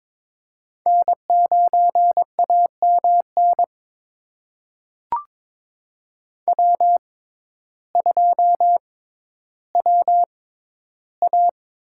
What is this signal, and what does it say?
Morse code, heard as a keyed tone, about 22 wpm, spaced as for 12 wpm, 700 Hz. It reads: N9AMN W2WA
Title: USA Call Signs - 6x Character Spacing - 22wpm